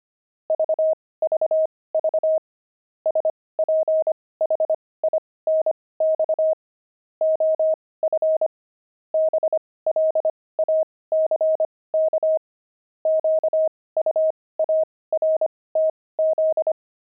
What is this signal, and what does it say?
Morse code, heard as a keyed tone, about 25 wpm, spaced as for 18 wpm, 630 Hz. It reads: VVV SPHINX OF BLACK QUARTZ
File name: pangram-fs-18-cs-25-sphinx